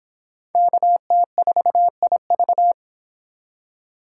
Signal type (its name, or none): Morse code